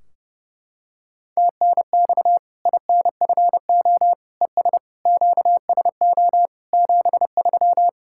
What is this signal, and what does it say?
Morse code, heard as a keyed tone, about 32 wpm, 700 Hz. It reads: TNX INFO ES QSO 73
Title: Practice QSO - Exchange 4 - Ending - 30wpm